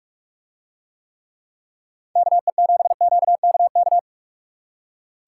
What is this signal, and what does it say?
Morse code, heard as a keyed tone, about 44 wpm, 700 Hz. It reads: KE7QKK